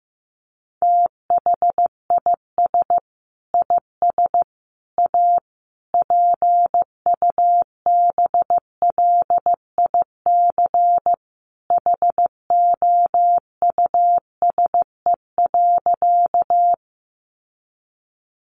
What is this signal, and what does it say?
Morse code, heard as a keyed tone, about 15 wpm, 700 Hz. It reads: THIS IS A PUBLIC HOUSE.
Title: Sentences from Top 600 Words 15wpm